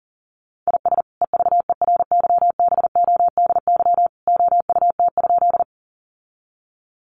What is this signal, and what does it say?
Morse code, heard as a keyed tone, about 40 wpm, 700 Hz. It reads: IS EVERYBODY OUT?
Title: Sentences from Top 1400 Words 40wpm